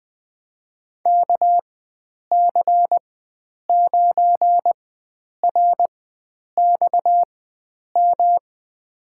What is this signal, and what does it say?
Morse code, heard as a keyed tone, about 20 wpm, spaced as for 9 wpm, 700 Hz. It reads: KC9RXM
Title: USA Call Signs - 6 Characters - 4x Character Spacing - 20wpm